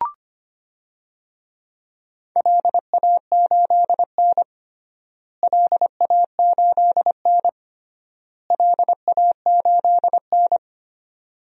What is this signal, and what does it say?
Morse code, heard as a keyed tone, about 25 wpm, 700 Hz. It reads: LA8N LA8N LA8N
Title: DXCC-100 Call Signs - 4 Characters - Repeated 3x - 25wpm